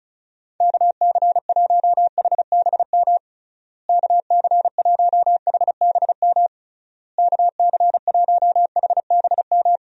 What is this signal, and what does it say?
Morse code, heard as a keyed tone, about 34 wpm, 700 Hz. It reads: KC1HBM KC1HBM KC1HBM
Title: USA Call Signs - 6 Characters - Repeated 3x - 35wpm